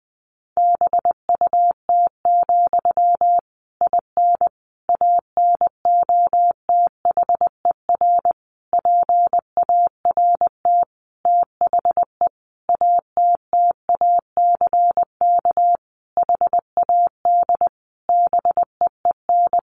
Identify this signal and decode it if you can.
Morse code, heard as a keyed tone, about 20 wpm, 700 Hz. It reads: BUT, IN ANOTHER PART THE ATTACK HAD BEEN
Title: Sentences from Top 1500 Words 20wpm